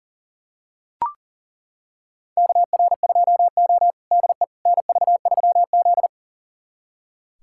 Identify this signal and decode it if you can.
Morse code, heard as a keyed tone, about 40 wpm, 700 Hz. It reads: KR2O DE NV3Z
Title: USA Callsign DE Callsign - 4 Characters - 40wpm